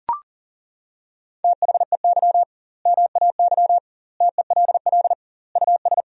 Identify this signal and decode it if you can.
Morse code, heard as a keyed tone, about 40 wpm, 700 Hz. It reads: THEY MAY TELL US
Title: 2x Word-Spacing - Sentences from Top 300 Words - 40wpm